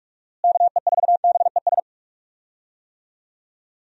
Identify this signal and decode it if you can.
Morse code, heard as a keyed tone, about 46 wpm, 700 Hz. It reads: KE4BES
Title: USA Call Signs - 6 Characters - Repeated 2x - 45wpm